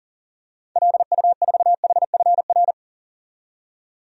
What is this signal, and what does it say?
Morse code, heard as a keyed tone, about 40 wpm, 700 Hz. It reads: LU4HFR
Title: DXCC-100 Call Signs - 6 Characters - 40wpm